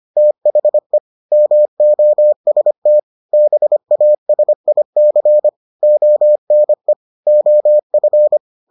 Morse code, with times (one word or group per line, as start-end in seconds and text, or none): 0.2-1.0 THE
1.3-3.0 MOST
3.3-5.5 BASIC
5.8-6.9 ONE
7.3-8.4 OF